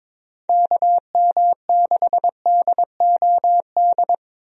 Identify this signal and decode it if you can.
Morse code, heard as a keyed tone, about 22 wpm, 700 Hz. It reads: KM6DOD
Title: USA Call Signs - 6 Characters - 22wpm